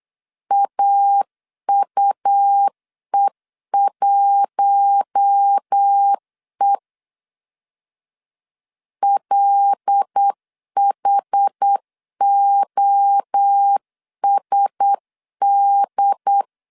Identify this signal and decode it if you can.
Morse code, heard as a keyed tone, about 8 wpm, 800 Hz. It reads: AUE1E LHOSD
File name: CW / DET / 1 6 1500C S C 8wpm 5219